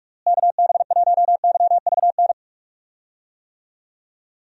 Morse code, heard as a keyed tone, about 44 wpm, 700 Hz. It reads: KB1YVN
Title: USA Call Signs - 6 Characters - Repeated 2x - 45wpm